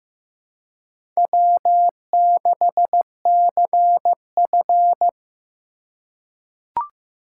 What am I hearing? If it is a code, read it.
Morse code, heard as a keyed tone, about 15 wpm, 700 Hz. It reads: W6CF
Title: USA Call Signs - 4 Characters - 2x Character Spacing - 15wpm